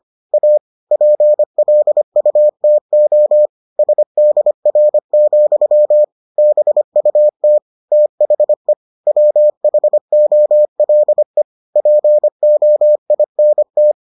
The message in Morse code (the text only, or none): A PLUTO SDR, BUT THE WHOLE POINT